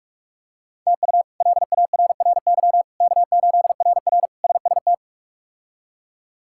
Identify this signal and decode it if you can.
Morse code, heard as a keyed tone, about 48 wpm, 700 Hz. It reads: TU LARRY K8RR SST
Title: K1USN SST QSO Practice - 45wpm